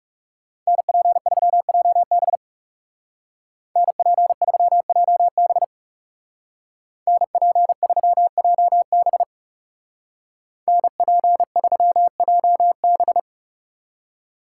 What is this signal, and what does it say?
Morse code, heard as a keyed tone, about 37 wpm, 700 Hz. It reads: NP3JB NP3JB NP3JB NP3JB
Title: Speed Racing - USA Call Signs 30wpm